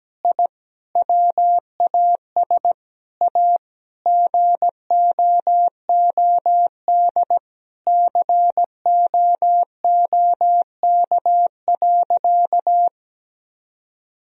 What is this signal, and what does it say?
Morse code, heard as a keyed tone, about 17 wpm, 700 Hz. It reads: I WAS A GOOD COOK.
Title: Sentences from Top 1000 Words - Review Part 4 - 17wpm